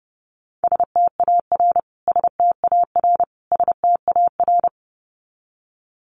Morse code, heard as a keyed tone, about 30 wpm, 700 Hz. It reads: STAR STAR STAR
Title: Top 500 Words Repeated 30wpm